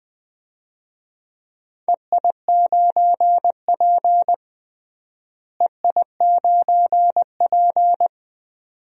Morse code, heard as a keyed tone, about 21 wpm, 700 Hz. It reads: EI9P EI9P
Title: DXCC-100 Call Signs - 4 Characters - Repeated 3x - 20wpm